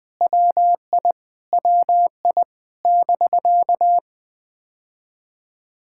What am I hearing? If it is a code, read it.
Morse code, heard as a keyed tone, about 20 wpm, 700 Hz. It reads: WI WI <BK>
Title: POTA QSO Practice - 20wpm